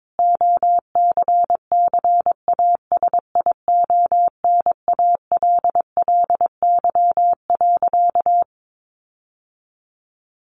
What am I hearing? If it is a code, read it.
Morse code, heard as a keyed tone, about 22 wpm, 700 Hz. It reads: OCCASIONALLY.
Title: Sentences from Top 1300 Words 22wpm